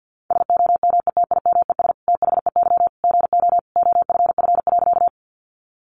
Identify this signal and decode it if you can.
Morse code, heard as a keyed tone, about 50 wpm, 700 Hz. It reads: SOMETIMES THEY GO OFF.
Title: Sentences from Top 600 Words 50wpm